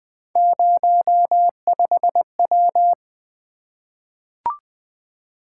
Morse code, heard as a keyed tone, about 20 wpm, 700 Hz. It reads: 05W
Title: Three Letter-Number Combinations - 20wpm